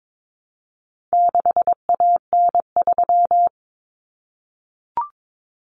Morse code, heard as a keyed tone, about 22 wpm, 700 Hz. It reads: 6AN3
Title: Common Cut Numbers - 4-Digit 22wpm